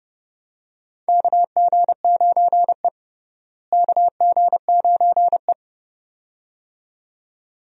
Morse code, heard as a keyed tone, about 29 wpm, 700 Hz. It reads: KG9E KG9E
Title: USA Call Signs - 4 Characters - Repeated 2x - 30wpm